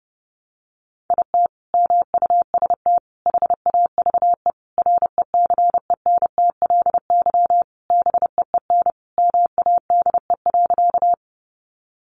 Morse code, heard as a keyed tone, about 30 wpm, 700 Hz. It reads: IT MUST HAVE RECENTLY BEEN MADE.
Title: Sentences from Top 1200 Words 30wpm